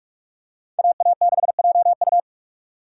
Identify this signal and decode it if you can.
Morse code, heard as a keyed tone, about 41 wpm, 700 Hz. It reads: AA6JU